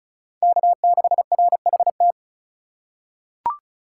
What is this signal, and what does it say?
Morse code, heard as a keyed tone, about 34 wpm, 700 Hz. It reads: K6RHT
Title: USA Call Signs - 2x Character Spacing - 35wpm